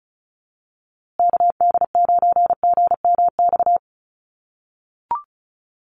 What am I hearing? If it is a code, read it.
Morse code, heard as a keyed tone, about 34 wpm, 700 Hz. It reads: KD9GMX
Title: USA Call Signs Repeated 3x 35wpm